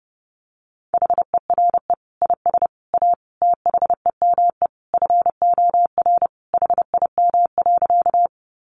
Morse code, heard as a keyed tone, about 30 wpm, 700 Hz. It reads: HERE IS A THEME FOR HIM.
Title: Sentences from Top 1100 Words 30wpm